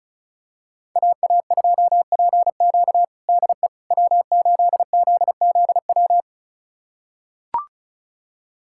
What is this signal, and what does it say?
Morse code, heard as a keyed tone, about 34 wpm, 700 Hz. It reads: AA2PQ DE W8ZZW
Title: USA Callsign DE Callsign 35wpm